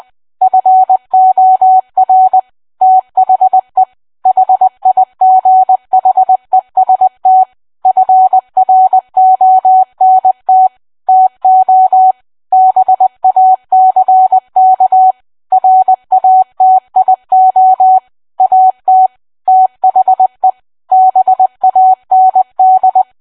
Morse code, heard as a keyed tone, about 20 wpm, 750 Hz. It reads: FOR THE HIGHEST FRONT TO BACK RATIO AT THE BAND